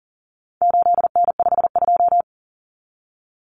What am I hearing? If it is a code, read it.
Morse code, heard as a keyed tone, about 40 wpm, 700 Hz. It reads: 8N52
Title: Common Cut Numbers - 4-Digit 40wpm